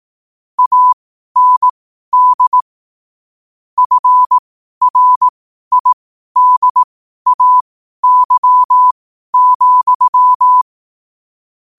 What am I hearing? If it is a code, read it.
Morse code, heard as a keyed tone, about 18 wpm, spaced as for 12 wpm, 1000 Hz. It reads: AND FRIDAY,